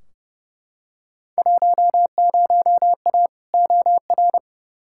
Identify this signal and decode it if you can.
Morse code, heard as a keyed tone, about 30 wpm, 700 Hz. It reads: 10A OR
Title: Field Day Exchanges - 30wpm